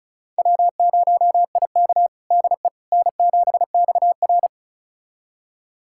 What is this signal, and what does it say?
Morse code, heard as a keyed tone, about 36 wpm, 700 Hz. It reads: W0IK DE N7XR
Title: USA Callsign DE Callsign - 4 Characters - 35wpm